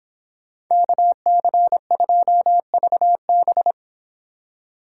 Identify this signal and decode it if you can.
Morse code, heard as a keyed tone, about 26 wpm, 700 Hz. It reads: KC2VB
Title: Speed Racing - USA 2x2 Call Signs 22wpm